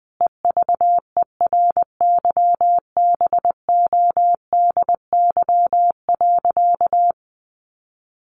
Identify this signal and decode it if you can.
Morse code, heard as a keyed tone, about 20 wpm, 700 Hz. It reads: EVERYBODY.
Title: Sentences from Top 1400 Words 20wpm